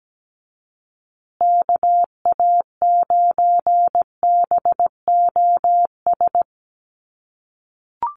Morse code, heard as a keyed tone, about 17 wpm, 700 Hz. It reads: KA9BOS
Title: USA Call Signs Repeated 3x 17wpm